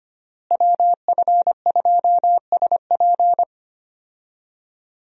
Morse code, heard as a keyed tone, about 25 wpm, 700 Hz. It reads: WF2SP
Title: USA Call Signs - 5 Characters - Repeated 3x - 25wpm